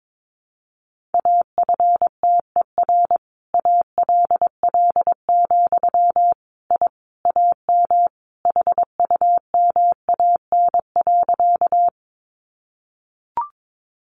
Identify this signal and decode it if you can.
Morse code, heard as a keyed tone, about 22 wpm, 700 Hz. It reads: AFTER ALL, I AM HUMAN.
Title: Sentences from Top 600 Words 22wpm